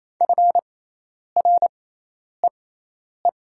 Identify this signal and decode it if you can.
Morse code, heard as a keyed tone, about 28 wpm, spaced as for 10 wpm, 700 Hz. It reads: FREE
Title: 6x Character Spacing - Top 500 Words - 28wpm